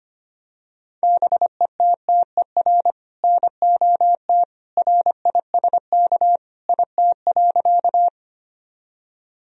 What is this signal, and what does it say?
Morse code, heard as a keyed tone, about 25 wpm, 700 Hz. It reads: BETTER NOT RISK IT.
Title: Sentences from Top 1000 Words - Review Part 4 - 25wpm